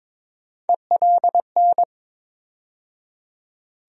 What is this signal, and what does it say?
Morse code, heard as a keyed tone, about 22 wpm, 700 Hz. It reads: ELN
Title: USA Call Sign Postfixes - 3 Letters 22wpm